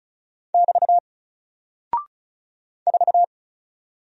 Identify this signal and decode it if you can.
Morse code, heard as a keyed tone, about 36 wpm, 700 Hz. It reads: <BT> 4
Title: Single Character (Letter:Number:Punctuation:Prosign) 35wpm